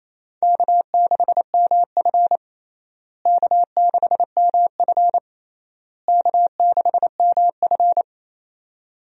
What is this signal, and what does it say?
Morse code, heard as a keyed tone, about 28 wpm, 700 Hz. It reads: K6MF K6MF K6MF
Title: USA Call Signs - 4 Characters - Repeated 3x - 28wpm